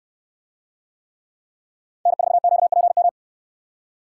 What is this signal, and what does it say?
Morse code, heard as a keyed tone, about 65 wpm, 700 Hz. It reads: N4QPM